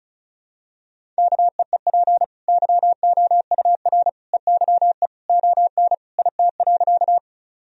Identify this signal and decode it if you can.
Morse code, heard as a keyed tone, about 34 wpm, 700 Hz. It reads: KEEP YOUR EYE ON IT.
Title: Sentences from Top 1000 Words - Review Part 1 - 35wpm